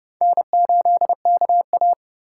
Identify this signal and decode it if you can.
Morse code, heard as a keyed tone, about 29 wpm, 700 Hz. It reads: N8KA
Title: USA Call Signs - 4 Characters - 8x Character Spacing - 30wpm